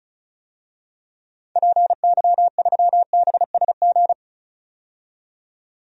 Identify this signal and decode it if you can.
Morse code, heard as a keyed tone, about 34 wpm, 700 Hz. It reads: PY3BSG
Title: DXCC-100 Call Signs - 6 Characters - 35wpm